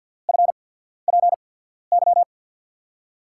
Morse code, heard as a keyed tone, about 51 wpm, spaced as for 14 wpm, 700 Hz. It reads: FPY
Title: USA Call Signs - 5 Characters - 8x Character Spacing - 50wpm